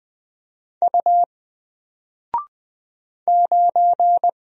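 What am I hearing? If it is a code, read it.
Morse code, heard as a keyed tone, about 20 wpm, 700 Hz. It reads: U 9
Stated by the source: Single letter and number Morse drill at 20 wpm, from N4LCD.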